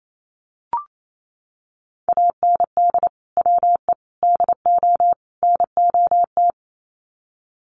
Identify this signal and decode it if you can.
Morse code, heard as a keyed tone, about 28 wpm, 700 Hz. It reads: AND WE DO NOT
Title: Sets of 4 Words 28wpm